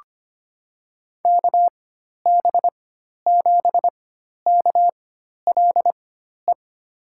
Morse code, heard as a keyed tone, about 25 wpm, spaced as for 12 wpm, 700 Hz. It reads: KB7KLE
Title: USA Call Signs - 6 Characters - 4x Character Spacing - 25wpm